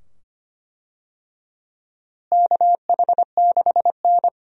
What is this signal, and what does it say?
Morse code, heard as a keyed tone, about 25 wpm, 700 Hz. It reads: KH6N
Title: DXCC-100 Call Signs - 4 Characters - Repeated 2x - 25wpm